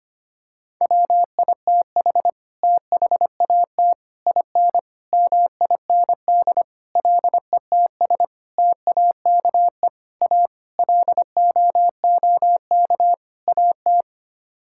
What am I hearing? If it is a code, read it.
Morse code, heard as a keyed tone, about 25 wpm, 700 Hz. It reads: WITH THAT IN MIND LETS TAKE A LOOK AT
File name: Sets of 9 Words - 25wpm